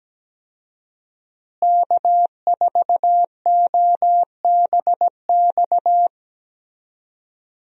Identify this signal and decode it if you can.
Morse code, heard as a keyed tone, about 17 wpm, 700 Hz. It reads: K4OBX
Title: USA Call Signs - 5 Characters - Repeated 2x - 17wpm